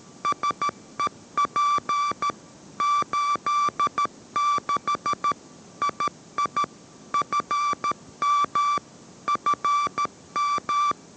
Morse code, heard as a keyed tone, about 13 wpm, 1160 Hz. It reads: SEP 86 II FM FM